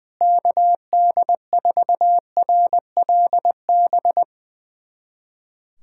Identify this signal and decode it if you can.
Morse code, heard as a keyed tone, about 21 wpm, 700 Hz. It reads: KD4RLB
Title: USA Callsign DE Callsign - 6 Characters - 20wpm